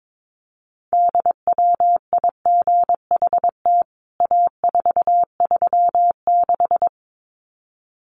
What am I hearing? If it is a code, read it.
Morse code, heard as a keyed tone, about 22 wpm, 700 Hz. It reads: DWIGHT A436
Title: CWT Exchange 22wpm